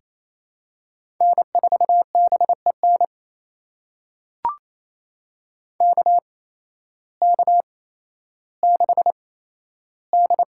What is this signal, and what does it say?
Morse code, heard as a keyed tone, about 28 wpm, spaced as for 13 wpm, 700 Hz. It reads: N4BEN KK6D
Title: USA Call Signs - 5 Characters - 8x Character Spacing - 28wpm